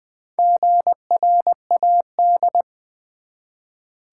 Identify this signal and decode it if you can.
Morse code, heard as a keyed tone, about 20 wpm, 700 Hz. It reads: GRAD
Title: Build A Word - 1 Letter At A Time - 3 to 5-Letter Words - 20wpm